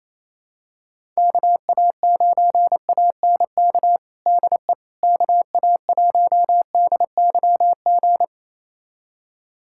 Morse code, heard as a keyed tone, about 28 wpm, 700 Hz. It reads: KA9ANK DE KA1DYG
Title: USA Callsign DE Callsign - 6 Characters - 28wpm